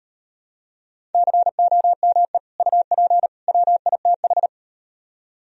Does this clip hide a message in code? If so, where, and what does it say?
Yes: Morse code from 1.1 to 4.5 s, heard as a keyed tone, about 38 wpm, 700 Hz. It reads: COME UP WITH